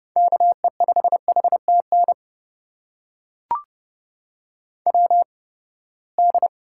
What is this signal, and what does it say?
Morse code, heard as a keyed tone, about 30 wpm, spaced as for 21 wpm, 700 Hz. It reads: KE5HTN WD
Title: USA Call Signs - 6 Characters - 8x Character Spacing - 30wpm